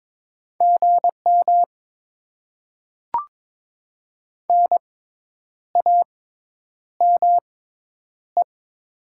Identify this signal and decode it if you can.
Morse code, heard as a keyed tone, about 22 wpm, spaced as for 9 wpm, 700 Hz. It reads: GM NAME
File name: Top 90 QSO Elements - 6x Character Spacing - 22wpm